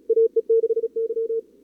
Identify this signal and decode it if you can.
Morse code, heard as a keyed tone, about 37 wpm, 440 Hz. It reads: AE6Y
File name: CWSprint / Feb2022 / k5tr-feb-na-sprint-cw-k5ot -0000